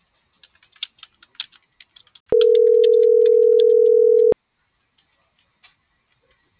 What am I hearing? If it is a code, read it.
Ringback tone (North America), 440 + 480 Hz.